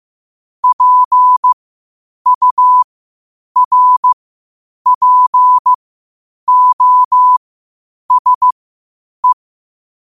Morse code, heard as a keyed tone, about 15 wpm, spaced as for 8 wpm, 1000 Hz. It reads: PURPOSE
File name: 8wpm